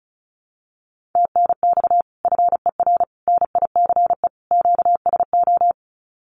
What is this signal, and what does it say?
Morse code, heard as a keyed tone, about 34 wpm, 700 Hz. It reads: TNX FER NICE QSO
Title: Common QSO Phrases Encore 35wpm